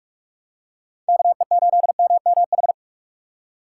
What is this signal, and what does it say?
Morse code, heard as a keyed tone, about 46 wpm, 700 Hz. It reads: KE8MMH
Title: USA Call Signs - 6 Characters - 3x Character Spacing - 45wpm